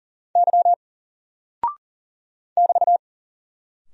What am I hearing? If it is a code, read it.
Morse code, heard as a keyed tone, about 40 wpm, 700 Hz. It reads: Y <BT>
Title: Single Character (Letter:Number:Punctuation:Prosign) 40wpm